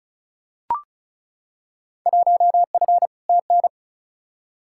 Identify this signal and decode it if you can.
Morse code, heard as a keyed tone, about 36 wpm, 700 Hz. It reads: 1F TN
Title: Field Day Exchanges - 35wpm